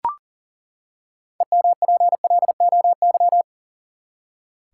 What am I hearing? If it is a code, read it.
Morse code, heard as a keyed tone, about 40 wpm, 700 Hz. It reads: EMPLOY
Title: Six Letter Words - 40wpm